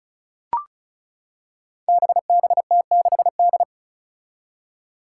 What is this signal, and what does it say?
Morse code, heard as a keyed tone, about 36 wpm, 700 Hz. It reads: BBT6D
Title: Five Challenging Letter-Number Combinations 35wpm